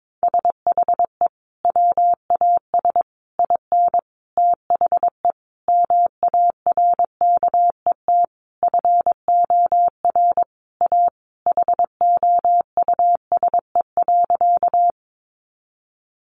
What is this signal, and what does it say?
Morse code, heard as a keyed tone, about 22 wpm, 700 Hz. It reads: SHE WAS IN THE MARKET FOR A HOUSE.
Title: Sentences from Top 600 Words 22wpm